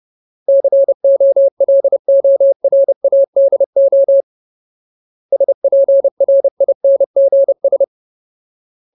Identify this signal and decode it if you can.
Morse code, heard as a keyed tone, about 30 wpm, 550 Hz. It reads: COLORADO SPRINGS
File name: Ultimate Word List / 30WPM Ultimate